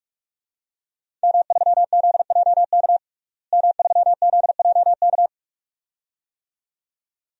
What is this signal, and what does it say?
Morse code, heard as a keyed tone, about 46 wpm, 700 Hz. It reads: M3ZJK M3ZJK
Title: DXCC-100 Call Signs - 5 Characters - Repeated 2x - 45wpm